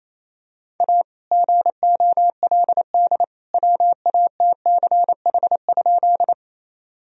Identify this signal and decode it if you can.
Morse code, heard as a keyed tone, about 28 wpm, 700 Hz. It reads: A GOLD WATCH?
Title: Sentences from Top 1000 Words - Review Part 3 - 28wpm